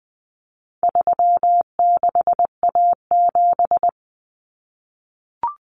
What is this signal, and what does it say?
Morse code, heard as a keyed tone, about 20 wpm, 700 Hz. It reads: 36A7
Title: Common Cut Numbers - 4-Digit 20wpm